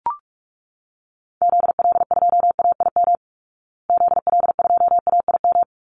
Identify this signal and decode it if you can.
Morse code, heard as a keyed tone, about 44 wpm, 700 Hz. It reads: ZL2AIM ZL2AIM
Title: Call Signs Repeated 3x - 45wpm